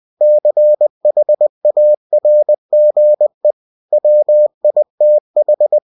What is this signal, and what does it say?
Morse code, heard as a keyed tone, about 20 wpm, 600 Hz. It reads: CHARGE WITH
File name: cw-books / winnie-the-pooh / winnie-20wpm-page-0114